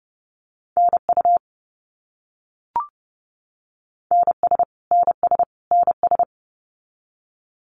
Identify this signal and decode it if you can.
Morse code, heard as a keyed tone, about 30 wpm, 700 Hz. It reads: NU NS NS NS
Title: Canadian Provinces and Territories - Repeated 3x - 30wpm